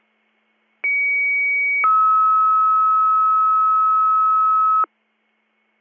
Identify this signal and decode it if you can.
Two-tone page, heard as a one-second tone, then a three-second tone, about 2250 then 1290 Hz. It is heard from 0.8 to 4.8 s.